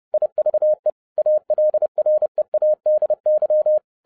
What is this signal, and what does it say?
Morse code, heard as a keyed tone, about 30 wpm, 610 Hz. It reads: IVE ALREADY
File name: 2012-01-mp3 / n12-I DIDNT MISS THE SOLAR CYCLE 24-30wpm